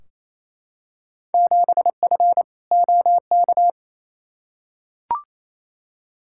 Morse code, heard as a keyed tone, about 28 wpm, 700 Hz. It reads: 7F OK
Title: Field Day Exchanges Encore - 28wpm